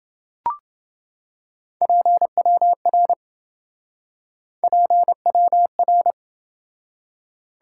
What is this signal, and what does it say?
Morse code, heard as a keyed tone, about 27 wpm, 700 Hz. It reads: PWR PWR